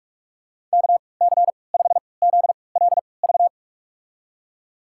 Morse code, heard as a keyed tone, about 47 wpm, spaced as for 26 wpm, 700 Hz. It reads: KC5ZLV
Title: USA Call Signs - 6 Characters - 3x Character Spacing - 45wpm